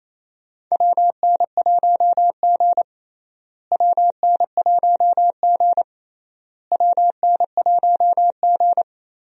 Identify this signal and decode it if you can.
Morse code, heard as a keyed tone, about 28 wpm, 700 Hz. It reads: WN1G WN1G WN1G
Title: USA Call Signs - 4 Characters - Repeated 3x - 28wpm